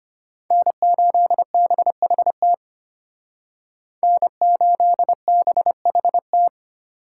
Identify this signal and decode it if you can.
Morse code, heard as a keyed tone, about 32 wpm, 700 Hz. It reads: N8BHT N8BHT